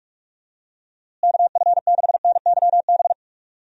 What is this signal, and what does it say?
Morse code, heard as a keyed tone, about 46 wpm, 700 Hz. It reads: KF6NYB
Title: USA Call Signs - 6 Characters - Repeated 3x - 45wpm